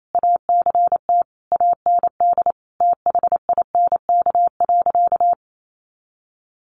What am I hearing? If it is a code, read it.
Morse code, heard as a keyed tone, about 28 wpm, 700 Hz. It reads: ACT AND THINK.
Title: Sentences from Top 500 Words 28wpm